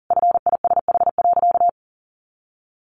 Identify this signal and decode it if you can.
Morse code, heard as a keyed tone, about 40 wpm, 700 Hz. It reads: FISH.
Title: Additional Easy Phrases 40wpm